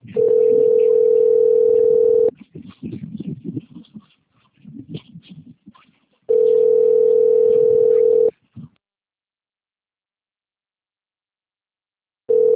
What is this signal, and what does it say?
Ringback tone (North America), 440 + 480 Hz.
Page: Americana - SP